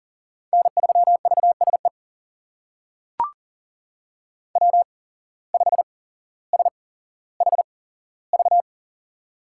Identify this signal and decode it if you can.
Morse code, heard as a keyed tone, about 40 wpm, spaced as for 17 wpm, 700 Hz. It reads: N3VSE W5SHV
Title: USA Call Signs - 5 Characters - 8x Character Spacing - 40wpm